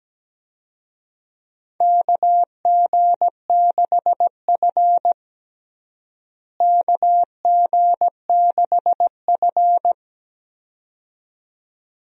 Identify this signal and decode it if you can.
Morse code, heard as a keyed tone, about 17 wpm, 700 Hz. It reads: KG6F KG6F
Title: USA Call Signs - 4 Characters - Repeated 2x - 17wpm